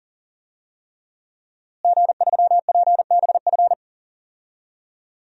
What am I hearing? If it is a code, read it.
Morse code, heard as a keyed tone, about 40 wpm, 700 Hz. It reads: G3PBF
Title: DXCC-100 Call Signs - 5 Characters - Repeated 3x - 40wpm